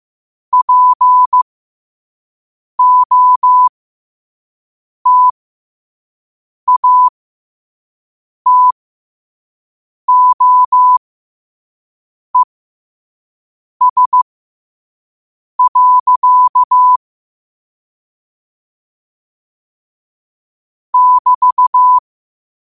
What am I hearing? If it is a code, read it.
Morse code, heard as a keyed tone, about 15 wpm, spaced as for 5 wpm, 1000 Hz. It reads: POTATOES. <BT>
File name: qotdfiles / 5wpm